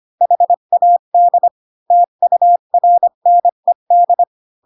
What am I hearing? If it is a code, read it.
Morse code, heard as a keyed tone, about 25 wpm, 700 Hz. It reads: HAD TURNED
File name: Pinocchio / Pinocchio ch1 20 25 ik1hge